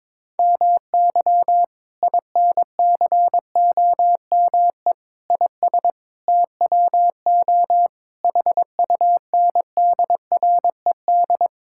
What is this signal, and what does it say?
Morse code, heard as a keyed tone, about 21 wpm, 700 Hz. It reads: MY INCOME IS TWO HUNDRED
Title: Sentences from Top 1000 Words - Review Part 5 - 22wpm